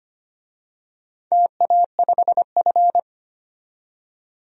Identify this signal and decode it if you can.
Morse code, heard as a keyed tone, about 25 wpm, 700 Hz. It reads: TA5F
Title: DXCC-100 Call Signs - 4 Characters - 25wpm